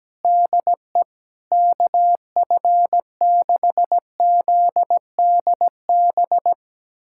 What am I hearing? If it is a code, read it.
Morse code, heard as a keyed tone, about 17 wpm, 700 Hz. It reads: DE KF6ZDB
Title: USA Callsign DE Callsign - 6 Characters - 17wpm